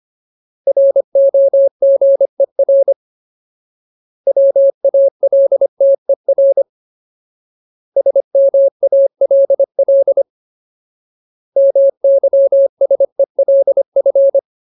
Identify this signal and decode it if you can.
Morse code, heard as a keyed tone, about 25 wpm, 550 Hz. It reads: ROGER WALTER SMALL MYSELF
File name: Ultimate Word List / 25WPM Ultimate